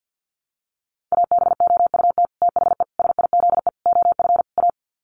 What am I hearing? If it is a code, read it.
Morse code, heard as a keyed tone, about 50 wpm, 700 Hz. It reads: ABOUT THE SIZE OF A
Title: Sets of 5 Words 50wpm